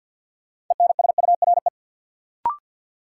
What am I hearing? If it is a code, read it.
Morse code, heard as a keyed tone, about 47 wpm, 700 Hz. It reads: ENSURE
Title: Sentences from Top 1100 Words - New Word Drill Alt - 50wpm